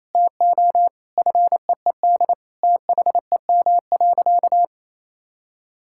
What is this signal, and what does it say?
Morse code, heard as a keyed tone, about 28 wpm, 700 Hz. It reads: TO FEED THEM.
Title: Sentences from Top 1600 Words 28wpm